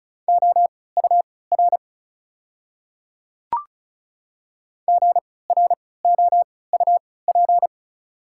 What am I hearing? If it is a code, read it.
Morse code, heard as a keyed tone, about 36 wpm, spaced as for 20 wpm, 690 Hz. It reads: OUR GROUP
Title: 3x Character Spacing - Top 300 Words - 35wpm